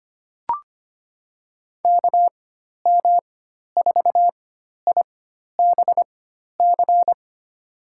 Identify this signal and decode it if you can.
Morse code, heard as a keyed tone, about 25 wpm, spaced as for 12 wpm, 700 Hz. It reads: KM4IBC
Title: USA Call Signs - 4x Character Spacing - 25wpm